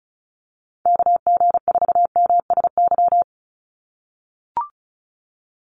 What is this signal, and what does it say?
Morse code, heard as a keyed tone, about 34 wpm, 700 Hz. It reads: KG4MSY
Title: USA Call Signs 35wpm